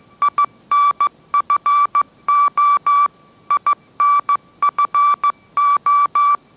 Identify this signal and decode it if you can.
Morse code, heard as a keyed tone, about 15 wpm, 1160 Hz. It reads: INFO INFO